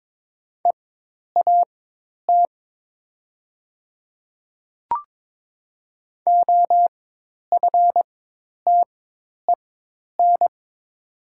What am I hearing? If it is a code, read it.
Morse code, heard as a keyed tone, about 22 wpm, spaced as for 10 wpm, 700 Hz. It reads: EAT OFTEN
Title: 4x Character Spacing - Top 500 Words - 22wpm